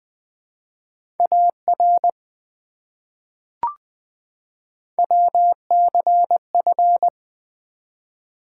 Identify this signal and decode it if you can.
Morse code, heard as a keyed tone, about 20 wpm, 700 Hz. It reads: AR WCF
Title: ARRL Sections v2 - 20wpm